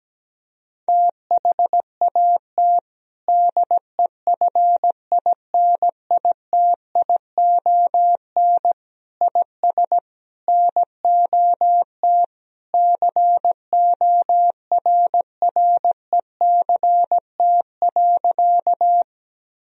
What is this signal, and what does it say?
Morse code, heard as a keyed tone, about 17 wpm, 700 Hz. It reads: THAT DEFINITION IS NOT CORRECT.
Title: Sentences from Top 1000 Words - Review Part 4 - 17wpm